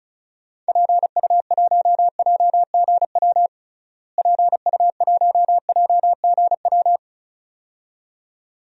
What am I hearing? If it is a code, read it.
Morse code, heard as a keyed tone, about 36 wpm, 700 Hz. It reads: PU1JGW PU1JGW
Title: DXCC-100 Call Signs - 6 Characters - Repeated 2x - 35wpm